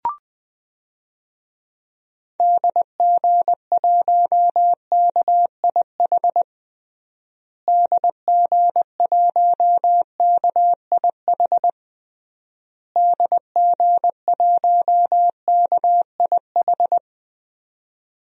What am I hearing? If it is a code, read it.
Morse code, heard as a keyed tone, about 20 wpm, 700 Hz. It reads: DG1KIH DG1KIH DG1KIH
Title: DXCC-100 Call Signs - 6 Characters - Repeated 3x - 20wpm